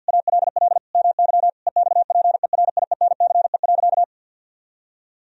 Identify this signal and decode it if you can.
Morse code, heard as a keyed tone, about 48 wpm, 700 Hz. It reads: ALL MY EXPERIENCE.
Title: Sentences from Top 1000 Words - Review Part 3 - 50wpm